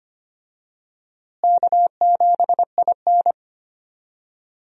Morse code, heard as a keyed tone, about 25 wpm, 700 Hz. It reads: K7IN
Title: USA Call Signs - 4 Characters - Repeated 2x - 25wpm